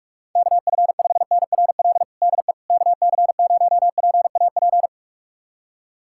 Morse code, heard as a keyed tone, about 42 wpm, 700 Hz. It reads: KF5NRL DE KC0PAP
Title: USA Callsign DE Callsign 45wpm